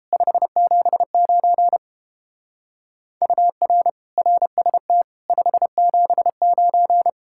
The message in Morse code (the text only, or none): 579 UR RST 579